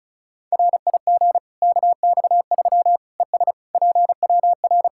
Morse code, heard as a keyed tone, about 36 wpm, 700 Hz. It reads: RIG KX3 ES PWR
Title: Practice QSO - Exchange 2 - Info - 35wpm